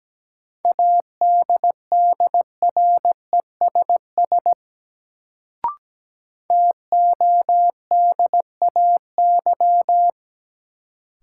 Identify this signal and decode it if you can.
Morse code, heard as a keyed tone, about 17 wpm, 700 Hz. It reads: ADDRESS TODAY
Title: Sets of 5 Words Encore - New Word Drill - 17wpm